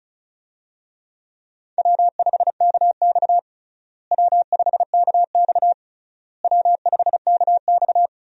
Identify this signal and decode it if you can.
Morse code, heard as a keyed tone, about 34 wpm, 700 Hz. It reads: W5KX W5KX W5KX
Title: USA Call Signs - 4 Characters - Repeated 3x - 35wpm